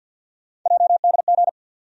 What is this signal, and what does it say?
Morse code, heard as a keyed tone, about 50 wpm, 700 Hz. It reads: JDG
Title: USA Call Sign Postfixes - 3 Letters 50wpm